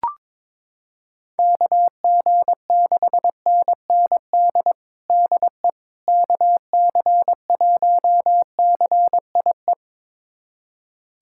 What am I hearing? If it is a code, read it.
Morse code, heard as a keyed tone, about 22 wpm, 700 Hz. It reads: KG6NND DE KC1CIE
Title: USA Callsign DE Callsign - 6 Characters - 22wpm